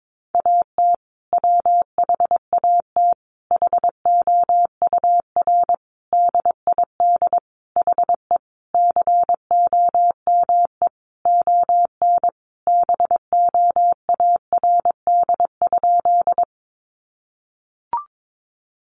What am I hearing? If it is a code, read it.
Morse code, heard as a keyed tone, about 22 wpm, 700 Hz. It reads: AT WHAT HOUR DID HE COME ON BOARD?
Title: Sentences from Top 700 Words 22wpm